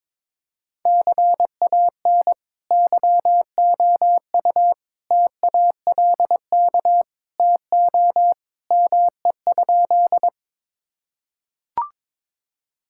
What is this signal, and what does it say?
Morse code, heard as a keyed tone, about 22 wpm, 700 Hz. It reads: CAN YOU TALK TO ME?
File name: practice / 22wpm / Sentences with 2-Word Phrases - 22wpm